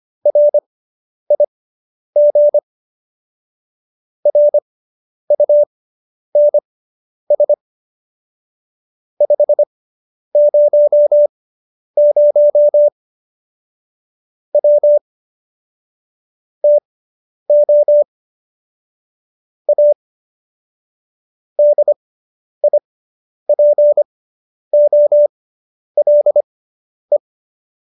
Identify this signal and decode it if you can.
Morse code, heard as a keyed tone, about 25 wpm, spaced as for 10 wpm, 590 Hz. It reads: RIG RUNS 500 W TO A DIPOLE